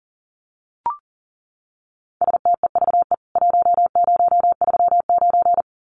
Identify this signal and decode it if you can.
Morse code, heard as a keyed tone, about 40 wpm, 700 Hz. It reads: STEVE 1039
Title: CWT Exchange 40wpm